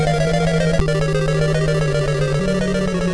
There is no code.